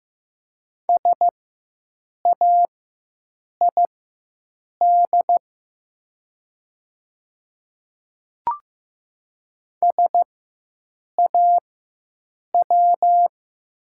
Morse code, heard as a keyed tone, about 15 wpm, spaced as for 7 wpm, 700 Hz. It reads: SAID SAW
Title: 4x Character Spacing - Top 300 Words - 15wpm